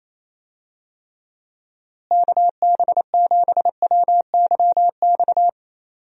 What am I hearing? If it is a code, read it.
Morse code, heard as a keyed tone, about 28 wpm, 700 Hz. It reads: KB7WYX
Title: USA Call Signs - 6 Characters - Repeated 2x - 28wpm